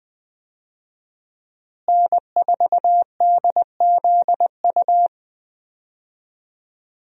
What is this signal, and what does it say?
Morse code, heard as a keyed tone, about 20 wpm, 700 Hz. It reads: N4DZU
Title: USA Call Signs - 5 Characters - Repeated 2x - 20wpm